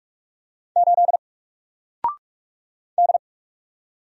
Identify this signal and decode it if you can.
Morse code, heard as a keyed tone, about 45 wpm, 700 Hz. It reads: 8 D
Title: Single Letter-Number Alt 45wpm